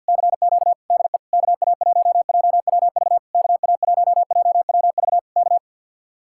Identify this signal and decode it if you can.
Morse code, heard as a keyed tone, about 65 wpm, spaced as for 57 wpm, 700 Hz. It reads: CQ DE KA1JWV KA1JWV K
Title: Speed-Racing Common QSO Phrases - 50wpm